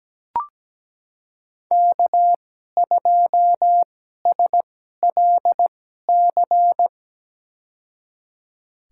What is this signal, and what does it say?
Morse code, heard as a keyed tone, about 17 wpm, spaced as for 12 wpm, 700 Hz. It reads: K2SLC
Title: USA Call Signs - 5 Characters - 2x Character Spacing - 17wpm